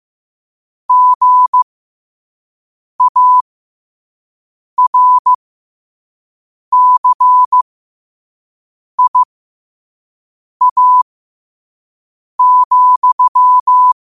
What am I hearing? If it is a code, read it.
Morse code, heard as a keyed tone, about 15 wpm, spaced as for 5 wpm, 1000 Hz. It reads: GARCIA,
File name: qotdfiles / 5wpm